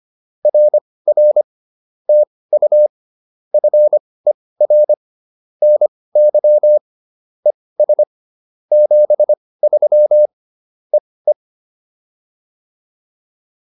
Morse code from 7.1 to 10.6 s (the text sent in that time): ES 73